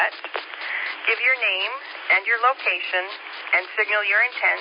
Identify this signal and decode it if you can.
Emergency Alert System (SAME) tones, heard as harsh buzzing bursts, with speech. It is heard from 0.5 to 3.9 s.